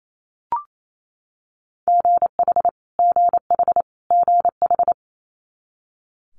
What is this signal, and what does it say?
Morse code, heard as a keyed tone, about 28 wpm, 700 Hz. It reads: GH GH GH
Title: Multi-letter Phonemes Repeated 3x 28wpm